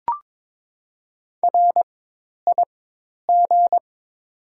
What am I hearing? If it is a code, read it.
Morse code, heard as a keyed tone, about 21 wpm, spaced as for 10 wpm, 700 Hz. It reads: RIG
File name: Top 90 QSO Elements - 4x Character Spacing - 22wpm